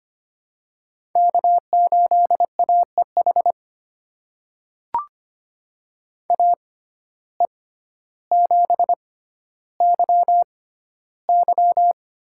Morse code, heard as a keyed tone, about 25 wpm, spaced as for 13 wpm, 700 Hz. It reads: K8AEH AE7YY
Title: USA Call Signs - 5 Characters - 6x Character Spacing - 25wpm